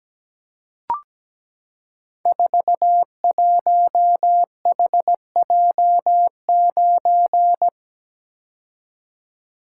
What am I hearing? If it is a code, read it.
Morse code, heard as a keyed tone, about 17 wpm, 700 Hz. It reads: 41HJ9
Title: Code Groups - 5 Letter:Numbers - 17wpm